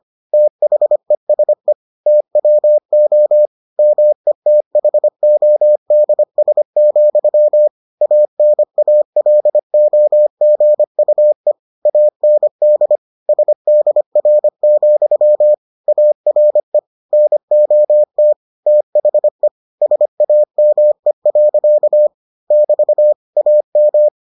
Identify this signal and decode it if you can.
Morse code, heard as a keyed tone, about 25 wpm, 600 Hz. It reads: THESE TWO METHODS, ANALOGUE AND SDR, ARE NOT THE SAME. <BT> AM